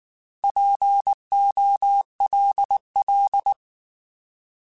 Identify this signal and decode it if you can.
Morse code, heard as a keyed tone, about 19 wpm, 800 Hz. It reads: POLL
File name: Alphabetical / Alphabetical mp3s / Ps @18